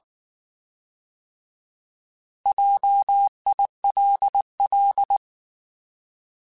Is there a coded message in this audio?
Yes: Morse code from 2.5 to 5.2 s, heard as a keyed tone, about 19 wpm, 800 Hz. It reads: JILL